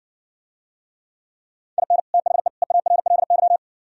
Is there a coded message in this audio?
Yes: Morse code from 1.8 to 3.6 s, heard as a keyed tone, about 57 wpm, 700 Hz. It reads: IN THE EARLY